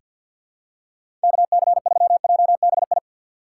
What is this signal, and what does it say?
Morse code, heard as a keyed tone, about 46 wpm, 700 Hz. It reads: KC3JBI
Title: USA Call Signs - 6 Characters - 4x Character Spacing - 50wpm